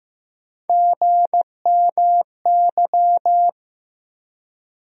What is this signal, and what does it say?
Morse code, heard as a keyed tone, about 15 wpm, 700 Hz. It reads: GMY
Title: USA Call Sign Postfixes - 3 Letters 15wpm